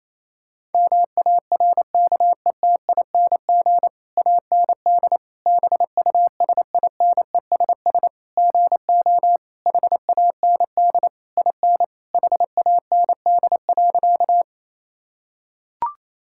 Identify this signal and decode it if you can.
Morse code, heard as a keyed tone, about 28 wpm, 700 Hz. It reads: MARKETING AND BUSINESS GO HAND IN HAND.
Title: Sentences from Top 1000 Words - Review Part 5 - 28wpm